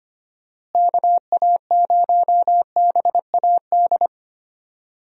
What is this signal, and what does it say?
Morse code, heard as a keyed tone, about 25 wpm, 700 Hz. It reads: KA0BAD
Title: USA Call Signs - 6 Characters - Repeated 2x - 25wpm